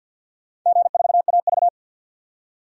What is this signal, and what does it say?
Morse code, heard as a keyed tone, about 46 wpm, 700 Hz. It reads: G4AV
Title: DXCC-100 Call Signs - 4 Characters - 50wpm